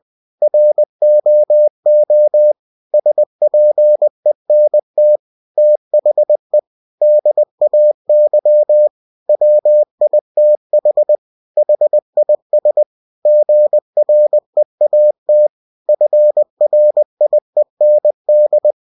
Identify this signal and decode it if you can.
Morse code, heard as a keyed tone, about 20 wpm, 600 Hz. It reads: ROO SPENT THE DAY WITH HIS GREAT FRIEND